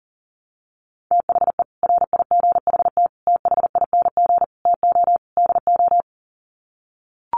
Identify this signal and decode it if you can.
Morse code, heard as a keyed tone, about 40 wpm, 700 Hz. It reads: THE RIGHT THING TO DO
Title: Sets of 5 Words 40wpm